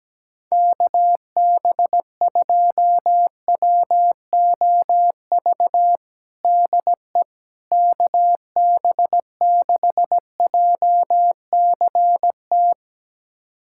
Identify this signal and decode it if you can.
Morse code, heard as a keyed tone, about 17 wpm, 700 Hz. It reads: KB2WOV DE KB6JCT
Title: USA Callsign DE Callsign - 6 Characters - 17wpm